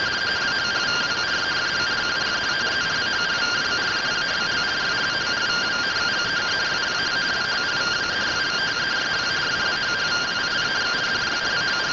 Two-tone teleprinter signal, about 170 Hz shift.
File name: findings / SAGITTAL 43mm 4mm.gen-03 rtty rx